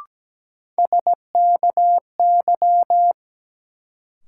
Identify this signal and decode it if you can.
Morse code, heard as a keyed tone, about 17 wpm, 700 Hz. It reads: SKY
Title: Sentences from Top 1200 Words - New Word Drill - 17wpm